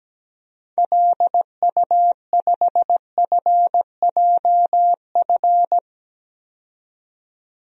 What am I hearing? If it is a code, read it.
Morse code, heard as a keyed tone, about 17 wpm, 700 Hz. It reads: LU5FJF
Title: DXCC-100 Call Signs - 6 Characters - 17wpm